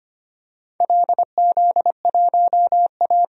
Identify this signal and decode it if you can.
Morse code, heard as a keyed tone, about 25 wpm, 690 Hz. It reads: LZ1A
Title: DXCC-100 Call Signs - 4 Characters - 25wpm